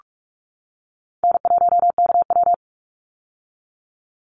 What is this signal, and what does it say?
Morse code, heard as a keyed tone, about 44 wpm, 700 Hz. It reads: N1KW
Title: Easy Call Signs - 45wpm